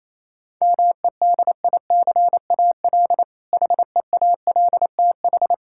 Morse code, heard as a keyed tone, about 27 wpm, 700 Hz. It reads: MEDICAL HEALTH
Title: K1USN SST QSO Practice 28wpm